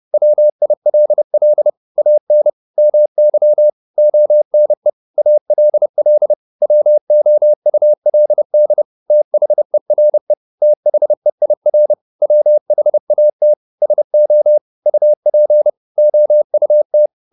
Morse code, heard as a keyed tone, about 30 wpm, 600 Hz. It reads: WILL AN MY ONE ALL WOULD THERE THEIR WHAT SO UP OUT
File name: Top-100-Words-30